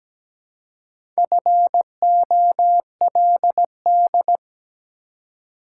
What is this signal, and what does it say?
Morse code, heard as a keyed tone, about 17 wpm, 700 Hz. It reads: FOLD
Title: Four Letter Words - 17wpm